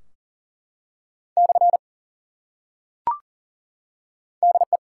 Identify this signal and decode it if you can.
Morse code, heard as a keyed tone, about 42 wpm, 700 Hz. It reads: / DE
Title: Top 180 QSO Elements 40wpm